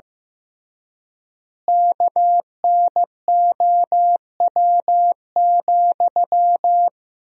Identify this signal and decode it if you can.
Morse code, heard as a keyed tone, about 16 wpm, 700 Hz. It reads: KNOW,
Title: 3x Word-Spacing - Sentences from Top 200 Words - 15wpm